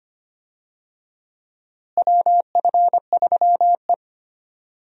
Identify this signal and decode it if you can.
Morse code, heard as a keyed tone, about 25 wpm, 700 Hz. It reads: WF3E
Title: USA Call Signs - 4 Characters - Repeated 2x - 25wpm